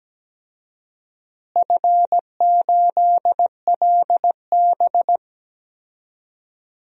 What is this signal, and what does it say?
Morse code, heard as a keyed tone, about 17 wpm, 700 Hz. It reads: F8LB
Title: DXCC-100 Call Signs - 4 Characters - Repeated 3x - 17wpm